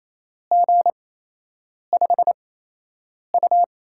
Morse code, heard as a keyed tone, about 28 wpm, 700 Hz. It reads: G 5 U
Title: USA Call Signs - 4 Characters - 8x Character Spacing - 28wpm